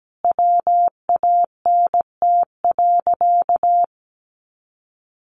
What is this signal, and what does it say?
Morse code, heard as a keyed tone, about 17 wpm, 700 Hz. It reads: WANT.
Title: Sentences from Top 1300 Words 17wpm